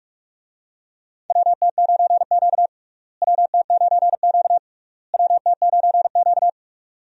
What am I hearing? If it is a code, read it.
Morse code, heard as a keyed tone, about 46 wpm, 700 Hz. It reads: WT9Q WT9Q WT9Q
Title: USA Call Signs - 4 Characters - Repeated 3x - 45wpm